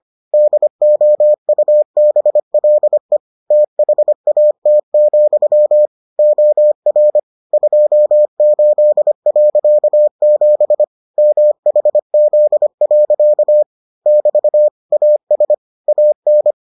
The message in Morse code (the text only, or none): DOUBLE THAT, OR 28.7 MHZ. <BT> AS AN